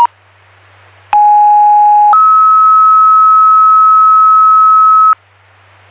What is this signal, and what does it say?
Two-tone page, heard as a one-second tone, then a three-second tone, about 830 then 1290 Hz. It is heard from 1.1 to 5.1 s.